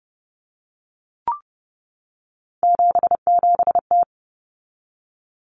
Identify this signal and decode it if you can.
Morse code, heard as a keyed tone, about 30 wpm, 700 Hz. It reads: 77T
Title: Common Cut Numbers - 3-Digit 30wpm